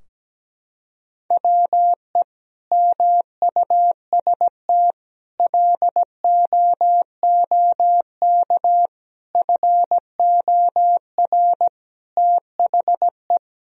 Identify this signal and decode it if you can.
Morse code, heard as a keyed tone, about 17 wpm, 700 Hz. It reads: WE MUST LOOK FOR THE
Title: Sentences from Top 1000 Words - Review Part 2 - 17wpm